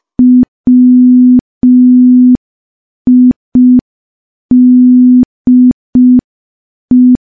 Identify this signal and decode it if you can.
Morse code, heard as a keyed tone, about 5 wpm, 250 Hz. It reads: WIDE
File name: morse / morse-training-05WPM